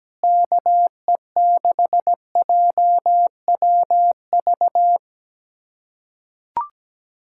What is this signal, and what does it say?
Morse code, heard as a keyed tone, about 17 wpm, 700 Hz. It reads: KE6JWV
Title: USA Callsign DE Callsign 17wpm